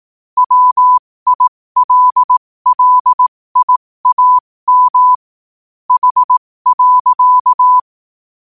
Morse code, heard as a keyed tone, about 18 wpm, spaced as for 15 wpm, 1000 Hz. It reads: WILLIAM H.